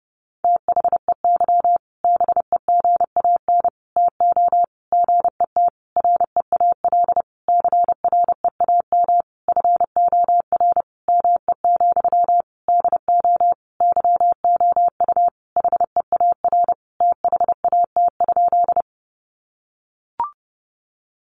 Morse code, heard as a keyed tone, about 30 wpm, 700 Hz. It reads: THEY BEGAN TO GET REAL CREAM FOR ME, DO YOU HEAR THAT?
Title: Sentences from Top 1500 Words 30wpm